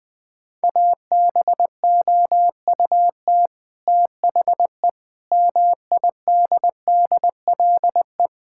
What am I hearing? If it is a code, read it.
Morse code, heard as a keyed tone, about 20 wpm, 700 Hz. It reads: ABOUT THE MIDDLE
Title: Sentences from Top 1000 Words - Review Part 6 - 20wpm